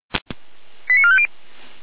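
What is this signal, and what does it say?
Five-tone selective call (ZVEI), heard as quick runs of five tones, tones of about 70 ms. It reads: 89450
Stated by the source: ZVEI five-tone call 89315 received by SDR, archived by feuerwehr-orlamuende.org fire brigade server.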